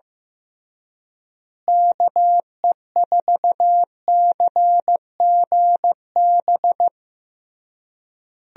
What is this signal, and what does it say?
Morse code, heard as a keyed tone, about 15 wpm, 700 Hz. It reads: KE4CGB
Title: USA Call Signs - 6 Characters - Repeated 3x - 15wpm